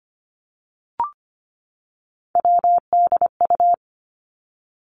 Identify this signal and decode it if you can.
Morse code, heard as a keyed tone, about 25 wpm, 700 Hz. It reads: WDU
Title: Three Challenging Letter-Number Combinations 25wpm